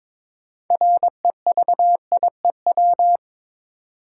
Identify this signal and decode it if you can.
Morse code, heard as a keyed tone, about 22 wpm, 700 Hz. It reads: REVIEW
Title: Sentences from Top 900 Words - New Word Drill - 22wpm